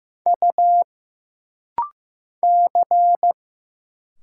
Morse code, heard as a keyed tone, about 15 wpm, 700 Hz. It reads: U C
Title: Single Letter-Number Alt - Rapid-Fire 15wpm